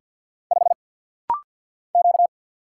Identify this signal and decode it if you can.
Morse code, heard as a keyed tone, about 50 wpm, 700 Hz. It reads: 5 Q
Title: Single Letter-Number Alt - Rapid-Fire 50wpm